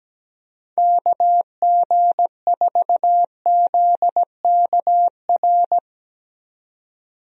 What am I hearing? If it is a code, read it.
Morse code, heard as a keyed tone, about 17 wpm, 700 Hz. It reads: KG4ZKR
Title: USA Call Signs - 6 Characters - 17wpm